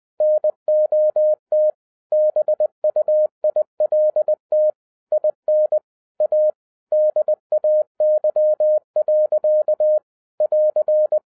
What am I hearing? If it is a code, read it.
Morse code, heard as a keyed tone, about 20 wpm, 610 Hz. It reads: NOT BUILT IN A DAY. <AR>
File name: 2011-10-mp3 / n13-TIPS FROM MY 34 YEARS OF EXPERIENCE 20wpm